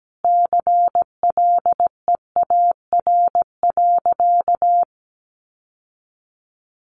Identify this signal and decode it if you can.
Morse code, heard as a keyed tone, about 17 wpm, 700 Hz. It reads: CLEAR.
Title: Sentences from Top 1200 Words 17wpm